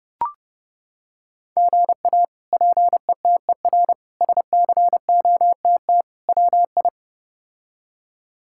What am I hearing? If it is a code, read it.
Morse code, heard as a keyed tone, about 29 wpm, 700 Hz. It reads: GA PETER SCOTT WI
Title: K1USN SST QSO Practice - 30wpm